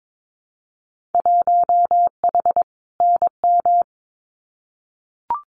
Morse code, heard as a keyed tone, about 22 wpm, 700 Hz. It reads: 1H NM
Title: Winter Field Day Exchanges 22wpm